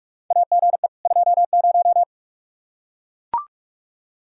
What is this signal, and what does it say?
Morse code, heard as a keyed tone, about 44 wpm, 700 Hz. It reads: AGE 20
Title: Practice QSO - Exchange 3 - More Info - 45wpm